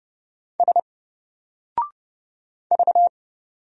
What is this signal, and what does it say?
Morse code, heard as a keyed tone, about 30 wpm, 700 Hz. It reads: S V
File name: Single-Letter-Number-30wpm